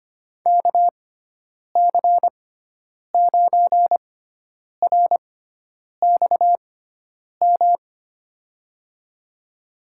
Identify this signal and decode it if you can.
Morse code, heard as a keyed tone, about 25 wpm, spaced as for 9 wpm, 700 Hz. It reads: KC9RXM
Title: USA Call Signs - 6 Characters - 6x Character Spacing - 25wpm